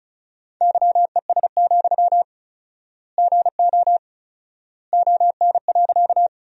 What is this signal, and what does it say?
Morse code, heard as a keyed tone, about 36 wpm, 700 Hz. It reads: YES, GO ON.
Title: 4x Word-Spacing - Sentences from Top 500 Words - 35wpm